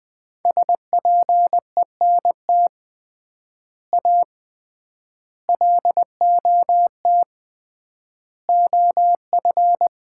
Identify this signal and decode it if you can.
Morse code, heard as a keyed tone, about 20 wpm, 700 Hz. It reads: SPENT A LOT OF
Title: 3x Word-Spacing - Sets of 5 Words - 20wpm